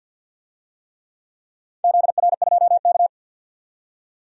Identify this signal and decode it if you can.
Morse code, heard as a keyed tone, about 51 wpm, 700 Hz. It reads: ZR2K
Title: DXCC-100 Call Signs - 4 Characters - 50wpm